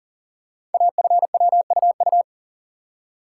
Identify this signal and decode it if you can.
Morse code, heard as a keyed tone, about 42 wpm, 700 Hz. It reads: AFWUU
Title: Code Groups - 5 Letters - 40wpm